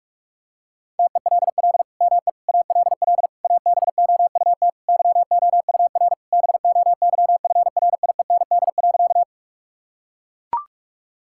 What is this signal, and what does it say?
Morse code, heard as a keyed tone, about 41 wpm, 700 Hz. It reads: TELL ME ALL ABOUT YOUR BOYFRIEND.
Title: Sentences from Top 1000 Words - Review Part 5 - 45wpm